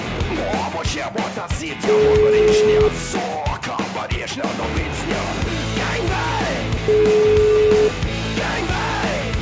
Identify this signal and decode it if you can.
Ringback tone (Europe, 425 Hz), with noise.